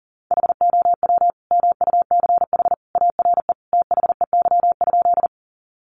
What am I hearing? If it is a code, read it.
Morse code, heard as a keyed tone, about 40 wpm, 700 Hz. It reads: HOW MUCH ARE THEY?
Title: Sentences from Top 200 Words 40wpm